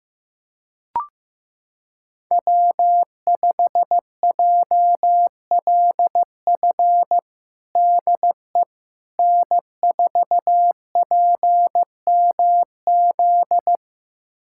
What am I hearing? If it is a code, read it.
Morse code, heard as a keyed tone, about 15 wpm, 700 Hz. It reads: W5JLF DE N4PMZ
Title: USA Callsign DE Callsign - 5 Characters - 15wpm